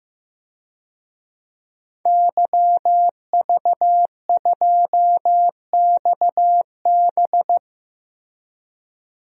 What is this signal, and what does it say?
Morse code, heard as a keyed tone, about 15 wpm, 700 Hz. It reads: YV2XB
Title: DXCC-100 Call Signs - 5 Characters - Repeated 2x - 15wpm